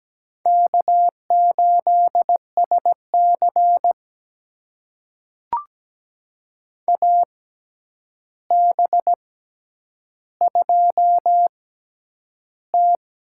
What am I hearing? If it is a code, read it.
Morse code, heard as a keyed tone, about 17 wpm, spaced as for 9 wpm, 700 Hz. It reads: K8SC AB2T
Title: USA Call Signs - 6x Character Spacing - 17wpm